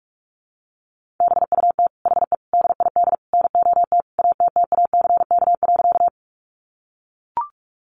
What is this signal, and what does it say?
Morse code, heard as a keyed tone, about 44 wpm, 700 Hz. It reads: BUT HE DID NOT ATTACK.
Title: Sentences from Top 1500 Words 45wpm